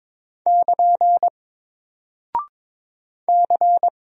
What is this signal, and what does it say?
Morse code, heard as a keyed tone, about 22 wpm, 700 Hz. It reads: <KN> C
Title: Single Character (Letter:Number:Punctuation:Prosign) Alt 22wpm